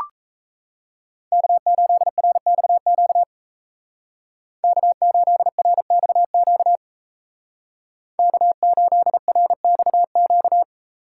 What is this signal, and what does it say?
Morse code, heard as a keyed tone, about 34 wpm, 700 Hz. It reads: K8RXQ K8RXQ K8RXQ
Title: Speed Racing - USA 1x3 Call Signs 28wpm